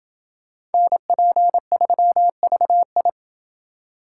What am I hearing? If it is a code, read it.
Morse code, heard as a keyed tone, about 28 wpm, 700 Hz. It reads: NP3VI